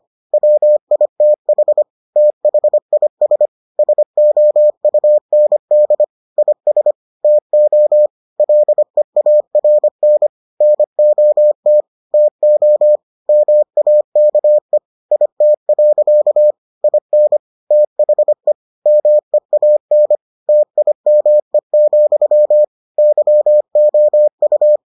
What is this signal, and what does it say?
Morse code, heard as a keyed tone, about 25 wpm, 600 Hz. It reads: WITH THIS SOUND IS TO LEARN NOT TO MAKE IT. IN THE MEAN TIME, YOU